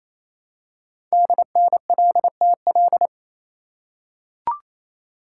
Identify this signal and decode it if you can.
Morse code, heard as a keyed tone, about 28 wpm, 700 Hz. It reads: DNLTL
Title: Code Groups - 5 Letters - 28wpm